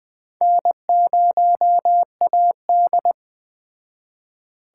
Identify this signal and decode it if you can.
Morse code, heard as a keyed tone, about 21 wpm, 700 Hz. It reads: N0AD